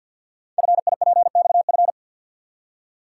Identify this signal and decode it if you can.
Morse code, heard as a keyed tone, about 48 wpm, 690 Hz. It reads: FIPXF
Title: Code Groups - 5 Letters - 50wpm